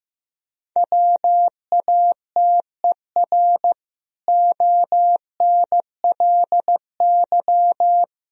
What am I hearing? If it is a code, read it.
Morse code, heard as a keyed tone, about 15 wpm, 700 Hz. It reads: WATER ONLY
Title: Sentences from Top 1000 Words - Review Part 4 - 15wpm